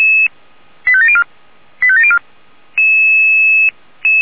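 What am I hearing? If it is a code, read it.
Five-tone selective call (ZVEI), heard as quick runs of five tones, tones of about 70 ms. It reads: 85704 85704
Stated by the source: ZVEI five-tone selective call received with an SDR.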